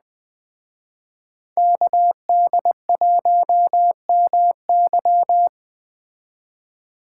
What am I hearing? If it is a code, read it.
Morse code, heard as a keyed tone, about 20 wpm, 700 Hz. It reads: KD1MY
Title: Speed Racing - USA 2x2 Call Signs 17wpm